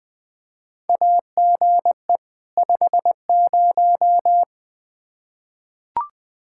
Morse code, heard as a keyed tone, about 20 wpm, 700 Hz. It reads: AGE 50
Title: K1USN SST QSO Practice 20wpm